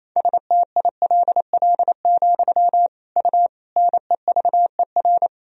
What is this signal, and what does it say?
Morse code, heard as a keyed tone, about 28 wpm, 700 Hz. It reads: STILL, U NEVER
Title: Sentences from Top 300 Words and Abbreviations 28wpm